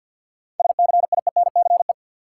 Morse code, heard as a keyed tone, about 53 wpm, 700 Hz. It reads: SCIENCE
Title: Sentences from Top 1400 Words - New Word Drill Alt - 50wpm